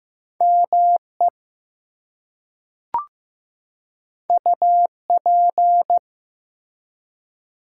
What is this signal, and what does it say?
Morse code, heard as a keyed tone, about 15 wpm, 700 Hz. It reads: ME UP
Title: Two Letter Words - 15wpm